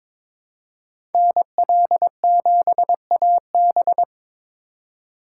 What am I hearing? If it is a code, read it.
Morse code, heard as a keyed tone, about 22 wpm, 700 Hz. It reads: NL7AB
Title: USA Call Signs - 6x Character Spacing - 22wpm